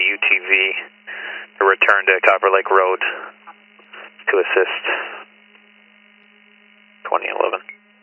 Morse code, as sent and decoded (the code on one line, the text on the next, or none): none